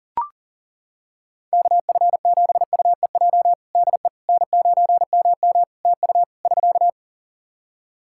Morse code, heard as a keyed tone, about 37 wpm, 700 Hz. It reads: KF7UEJ DE N9MM TU <SK>
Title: Practice QSO - Exchange 4 - Ending - 40wpm